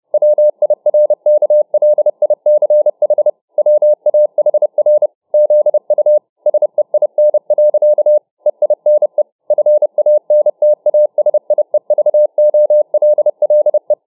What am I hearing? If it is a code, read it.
Morse code, heard as a keyed tone, about 30 wpm, 600 Hz. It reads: WIRKLICH WAHR ZU SEIN. EINE FANTASIEVOLLE